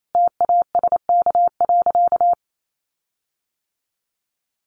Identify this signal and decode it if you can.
Morse code, heard as a keyed tone, about 28 wpm, 700 Hz. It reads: TASK.
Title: Sentences from Top 1300 Words 28wpm